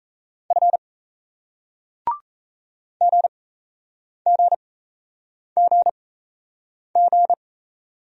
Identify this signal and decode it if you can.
Morse code, heard as a keyed tone, about 37 wpm, spaced as for 8 wpm, 700 Hz. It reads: F GGGG